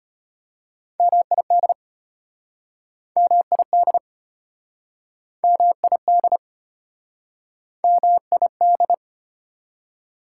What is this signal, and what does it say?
Morse code, heard as a keyed tone, about 35 wpm, 700 Hz. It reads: MID MID MID MID